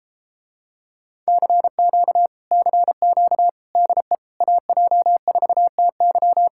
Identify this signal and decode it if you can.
Morse code, heard as a keyed tone, about 34 wpm, 700 Hz. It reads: CQ CQ DE AJ4TY